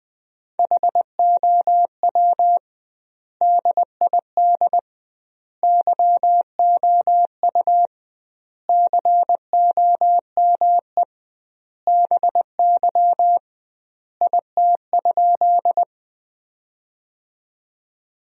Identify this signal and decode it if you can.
Morse code, heard as a keyed tone, about 21 wpm, 700 Hz. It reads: HOW DID YOU COME BY IT?
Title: 2x Word-Spacing - Sentences from Top 100 Words - 20wpm